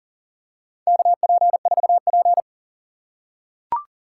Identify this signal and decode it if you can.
Morse code, heard as a keyed tone, about 40 wpm, 700 Hz. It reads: KP4P
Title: USA Call Signs - 4 Characters - 6x Character Spacing - 40wpm